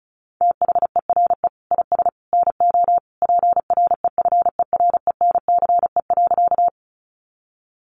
Morse code, heard as a keyed tone, about 34 wpm, 700 Hz. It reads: THERE IS NO PREFERENCE.
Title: Sentences from Top 1500 Words 35wpm